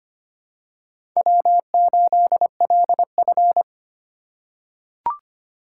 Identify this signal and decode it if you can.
Morse code, heard as a keyed tone, about 25 wpm, 700 Hz. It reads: W8LF
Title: USA Call Signs - 2x Character Spacing - 25wpm